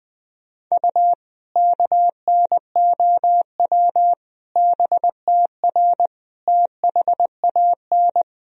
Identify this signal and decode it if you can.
Morse code, heard as a keyed tone, about 20 wpm, 700 Hz. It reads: U KNOW BTR THAN
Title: Sentences from Top 400 Words and Abbreviations 20wpm